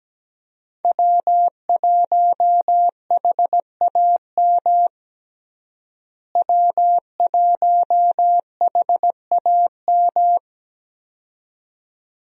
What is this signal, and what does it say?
Morse code, heard as a keyed tone, about 17 wpm, 700 Hz. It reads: W1HAM W1HAM
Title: USA Call Signs - 5 Characters - Repeated 2x - 17wpm